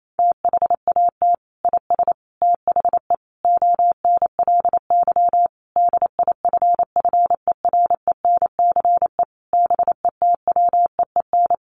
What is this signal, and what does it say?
Morse code, heard as a keyed tone, about 28 wpm, 700 Hz. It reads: THAT IS THE ONLY DIFFERENCE BETWEEN
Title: Sentences from Top 800 Words 28wpm